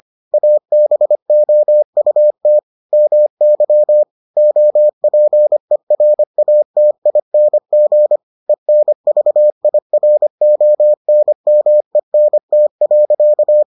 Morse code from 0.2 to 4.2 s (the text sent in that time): ABOUT MY